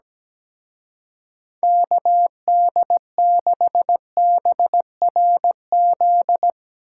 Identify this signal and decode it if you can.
Morse code, heard as a keyed tone, about 18 wpm, 700 Hz. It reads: KD6BRZ
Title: Speed Racing - USA 2x3 Call Signs 17wpm